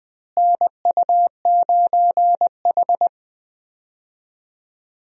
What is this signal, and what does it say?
Morse code, heard as a keyed tone, about 20 wpm, 700 Hz. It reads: NU9H
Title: USA Call Signs - 4 Characters - Repeated 3x - 20wpm